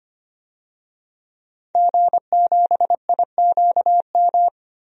Morse code, heard as a keyed tone, about 25 wpm, 700 Hz. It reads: G7IQM
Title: DXCC-100 Call Signs - 5 Characters - 25wpm